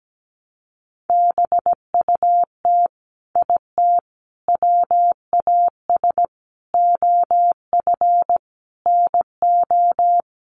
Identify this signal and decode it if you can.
Morse code, heard as a keyed tone, about 17 wpm, 700 Hz. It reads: BUT IT WAS OF NO
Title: Sentences from Top 100 Words 17wpm